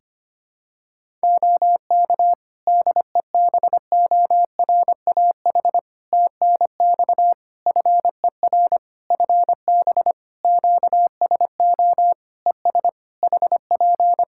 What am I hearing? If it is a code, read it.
Morse code, heard as a keyed tone, about 25 wpm, 700 Hz. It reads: OK DEBORAH TNX FER FB QSO ES HP
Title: Practice QSO - Exchange 4 - Ending - 25wpm